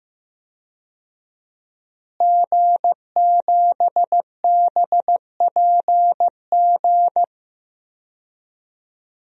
Morse code, heard as a keyed tone, about 15 wpm, 700 Hz. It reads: G7BPG
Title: DXCC-100 Call Signs - 5 Characters - Repeated 2x - 15wpm